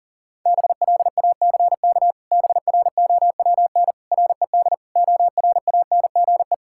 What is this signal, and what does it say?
Morse code, heard as a keyed tone, about 40 wpm, 700 Hz. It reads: BLACK BROWN RED ORANGE
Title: Pangram Challenge v2 - 40wpm